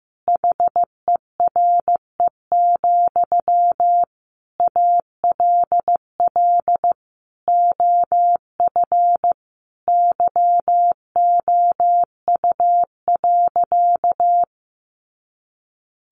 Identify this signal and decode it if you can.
Morse code, heard as a keyed tone, about 14 wpm, 700 Hz. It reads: HERE, ALL OF YOU.
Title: Sentences from Top 500 Words 15wpm